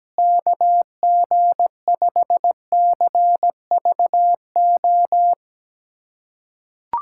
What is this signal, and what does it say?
Morse code, heard as a keyed tone, about 17 wpm, 700 Hz. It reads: KG5CVO
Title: USA Callsign DE Callsign - 6 Characters - 17wpm